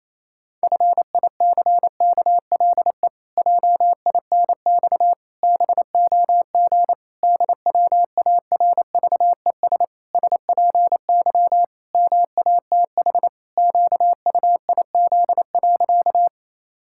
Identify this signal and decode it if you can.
Morse code, heard as a keyed tone, about 28 wpm, 700 Hz. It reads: FICKLE JINX BOG DWARVES SPY MATH QUIZ.
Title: Pangram Challenge v2 - 28wpm